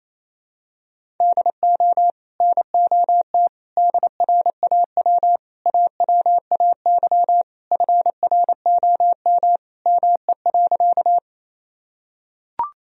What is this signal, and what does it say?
Morse code, heard as a keyed tone, about 28 wpm, 700 Hz. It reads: DO NOT DRAW AWAY FROM ME.
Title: Sentences from Top 1000 Words - Review Part 1 - 28wpm